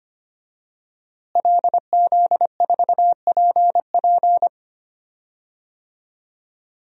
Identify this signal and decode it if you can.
Morse code, heard as a keyed tone, about 25 wpm, 700 Hz. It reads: LZ4PP
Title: DXCC-100 Call Signs - 5 Characters - Repeated 3x - 25wpm